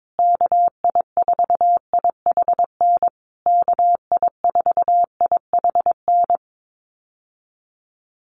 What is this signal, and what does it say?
Morse code, heard as a keyed tone, about 22 wpm, 700 Hz. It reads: KI4IHN KI4IHN
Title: USA Call Signs Repeated 3x 22wpm